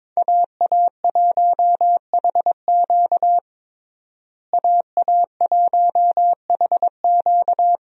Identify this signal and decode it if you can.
Morse code, heard as a keyed tone, about 22 wpm, 700 Hz. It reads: AA1HQ AA1HQ
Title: USA Call Signs - 5 Characters - Repeated 2x - 22wpm